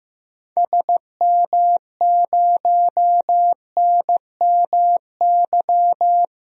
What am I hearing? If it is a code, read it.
Morse code, heard as a keyed tone, about 15 wpm, 700 Hz. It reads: SM0NMY
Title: DXCC-100 Call Signs - 6 Characters - Repeated 2x - 15wpm